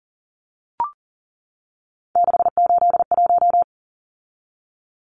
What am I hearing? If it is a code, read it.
Morse code, heard as a keyed tone, about 40 wpm, 700 Hz. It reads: 681
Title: Numbers - 3-Digit 40wpm